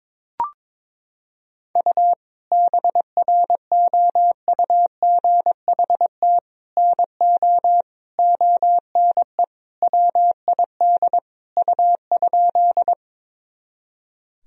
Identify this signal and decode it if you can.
Morse code, heard as a keyed tone, about 22 wpm, 700 Hz. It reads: U BROUGHT NO ONE WID U?
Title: Sentences from Top 500 Words and Abbreviations 22wpm